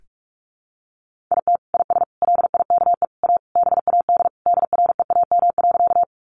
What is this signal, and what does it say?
Morse code, heard as a keyed tone, about 44 wpm, 700 Hz. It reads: IT IS LIKE A BAD DREAM.
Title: Sentences from Top 1500 Words 45wpm